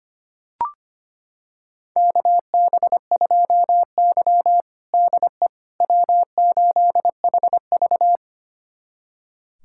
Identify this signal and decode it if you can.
Morse code, heard as a keyed tone, about 25 wpm, 700 Hz. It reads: KB2Y DE W8HV
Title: USA Callsign DE Callsign - 4 Characters - 25wpm